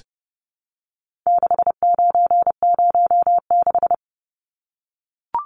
Morse code, heard as a keyed tone, about 30 wpm, 700 Hz. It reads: 6906
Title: Numbers - 4-Digit 30wpm